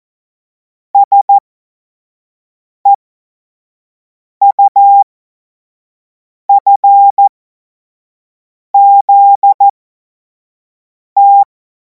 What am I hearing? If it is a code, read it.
Morse code, heard as a keyed tone, about 14 wpm, spaced as for 5 wpm, 800 Hz. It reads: SEUFZT
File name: Morsekurs / Witze / Witz 24. März 2026 25 F70